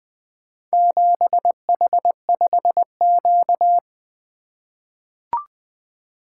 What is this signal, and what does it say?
Morse code, heard as a keyed tone, about 20 wpm, 700 Hz. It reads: 7H5Q
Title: Four Letter-Number Combinations - 20wpm